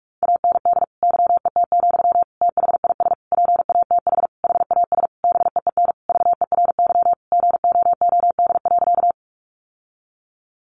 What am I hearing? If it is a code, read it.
Morse code, heard as a keyed tone, about 44 wpm, 700 Hz. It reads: AND YET, THIS PATH HAS BEEN VERY GOOD.
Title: Sentences from Top 1400 Words 45wpm